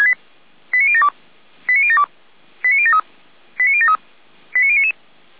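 Five-tone selective call (ZVEI), heard as quick runs of five tones, tones of about 70 ms. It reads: 89072 89072 89073 89073 89099
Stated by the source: ZVEI five-tone selective call received on SDR.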